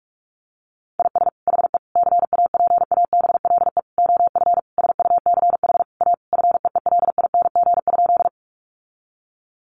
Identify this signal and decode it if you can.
Morse code, heard as a keyed tone, about 46 wpm, 700 Hz. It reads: IS HE CAPABLE OF SUCH A FEELING?
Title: Sentences from Top 1200 Words 45wpm